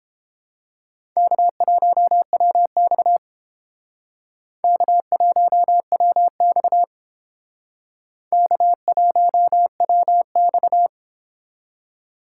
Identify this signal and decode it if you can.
Morse code, heard as a keyed tone, about 29 wpm, 700 Hz. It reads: K1WX K1WX K1WX